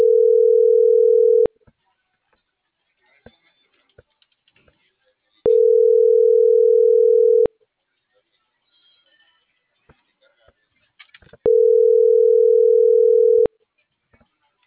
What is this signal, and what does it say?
Ringback tone (North America), 440 + 480 Hz.